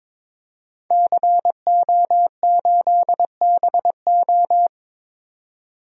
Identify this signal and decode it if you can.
Morse code, heard as a keyed tone, about 22 wpm, 700 Hz. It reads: CO8BO
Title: DXCC-100 Call Signs - 5 Characters - 22wpm